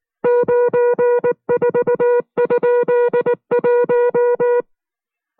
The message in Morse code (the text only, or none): 94?1